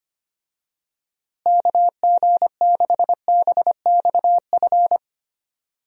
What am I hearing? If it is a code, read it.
Morse code, heard as a keyed tone, about 25 wpm, 700 Hz. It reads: KG6BXF
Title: USA Call Signs - 6 Characters - Repeated 3x - 25wpm